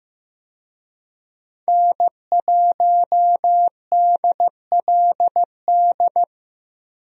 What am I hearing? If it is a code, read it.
Morse code, heard as a keyed tone, about 15 wpm, 700 Hz. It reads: N1DLD
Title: USA Call Signs - 5 Characters - Repeated 2x - 15wpm